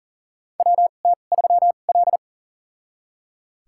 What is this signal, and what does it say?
Morse code, heard as a keyed tone, about 37 wpm, spaced as for 28 wpm, 690 Hz. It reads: WT3L
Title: USA Call Signs - 4 Characters - 2x Character Spacing - 40wpm